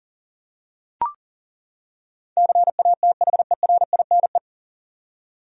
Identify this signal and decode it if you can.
Morse code, heard as a keyed tone, about 40 wpm, 700 Hz. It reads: CATHERINE
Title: USA Common Names - 40wpm